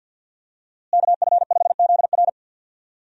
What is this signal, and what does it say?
Morse code, heard as a keyed tone, about 51 wpm, 700 Hz. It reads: KF5ZR
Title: USA Call Signs - 5 Characters - Repeated 2x - 50wpm